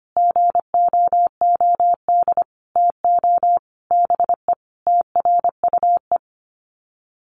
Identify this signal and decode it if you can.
Morse code, heard as a keyed tone, about 25 wpm, 700 Hz. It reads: GOOD TO BE TRUE
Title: Sets of 5 Words Encore 25wpm